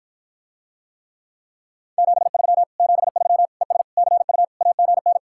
Morse code, heard as a keyed tone, about 54 wpm, 700 Hz. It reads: 73 73 ES CU AGN